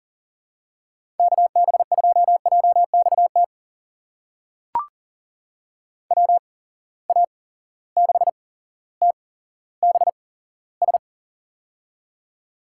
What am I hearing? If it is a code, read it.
Morse code, heard as a keyed tone, about 39 wpm, spaced as for 17 wpm, 700 Hz. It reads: KB2JXT WA6TBS
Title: USA Call Signs - 6 Characters - 8x Character Spacing - 40wpm